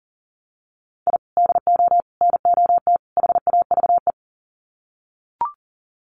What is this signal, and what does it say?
Morse code, heard as a keyed tone, about 40 wpm, 700 Hz. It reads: I DO NOT HAVE
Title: Sets of 4 Words 40wpm